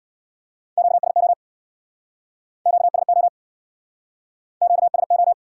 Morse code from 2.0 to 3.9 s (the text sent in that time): QSO